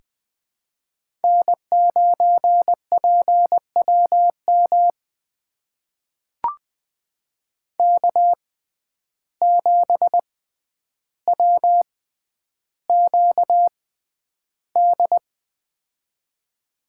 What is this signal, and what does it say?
Morse code, heard as a keyed tone, about 20 wpm, spaced as for 10 wpm, 700 Hz. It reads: N9PWM K7WQD
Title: USA Call Signs - 5 Characters - 6x Character Spacing - 20wpm